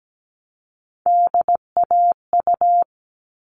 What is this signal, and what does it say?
Morse code, heard as a keyed tone, about 17 wpm, 700 Hz. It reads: DAU
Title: All Cut Numbers - 3-Digit 17wpm